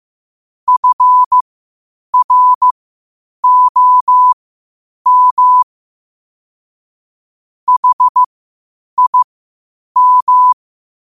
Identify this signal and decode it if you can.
Morse code, heard as a keyed tone, about 15 wpm, spaced as for 8 wpm, 1000 Hz. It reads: FROM HIM
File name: qotdfiles / 8wpm